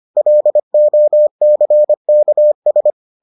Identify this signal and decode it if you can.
Morse code, heard as a keyed tone, about 25 wpm, 600 Hz. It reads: LOCKS